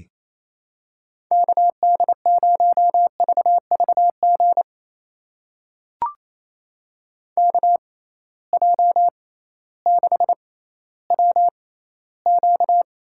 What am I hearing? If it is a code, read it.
Morse code, heard as a keyed tone, about 28 wpm, spaced as for 15 wpm, 700 Hz. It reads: KD0VVG KJ6WQ
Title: USA Call Signs - 6 Characters - 6x Character Spacing - 28wpm